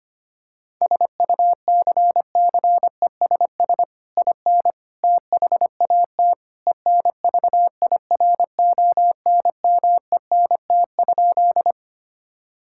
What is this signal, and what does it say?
Morse code, heard as a keyed tone, about 25 wpm, 700 Hz. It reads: SUCCESS IN THAT ENVIRONMENT?
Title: Sentences from Top 1000 Words - Review Part 5 - 25wpm